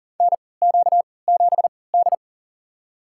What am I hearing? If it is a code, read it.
Morse code, heard as a keyed tone, about 40 wpm, spaced as for 23 wpm, 700 Hz. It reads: NQ7D
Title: USA Call Signs - 4 Characters - 3x Character Spacing - 40wpm